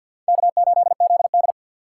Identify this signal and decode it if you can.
Morse code, heard as a keyed tone, about 53 wpm, 700 Hz. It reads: K8ZD
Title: USA Call Signs - 4x Character Spacing - 50wpm